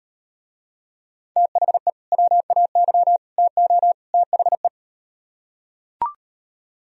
Morse code, heard as a keyed tone, about 38 wpm, 690 Hz. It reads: THE WAY TO THE